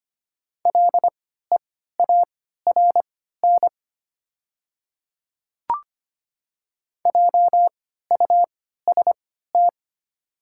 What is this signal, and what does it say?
Morse code, heard as a keyed tone, about 25 wpm, spaced as for 14 wpm, 700 Hz. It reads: LEARN JUST
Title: 3x Character Spacing - Top 300 Words - 25wpm